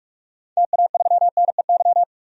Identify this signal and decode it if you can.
Morse code, heard as a keyed tone, about 46 wpm, 700 Hz. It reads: TA3NEY
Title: DXCC-100 Call Signs - 6 Characters - 45wpm